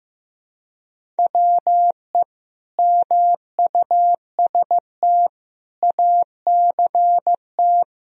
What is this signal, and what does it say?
Morse code, heard as a keyed tone, about 15 wpm, 700 Hz. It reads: WE MUST ACT
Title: Sentences from Top 1000 Words - Review Part 1 - 15wpm